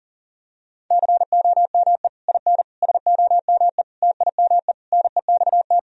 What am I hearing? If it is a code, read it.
Morse code, heard as a keyed tone, about 40 wpm, 700 Hz. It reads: COME IN SOME TIME NEXT
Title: Sentences from Top 1000 Words - Review Part 2 - 40wpm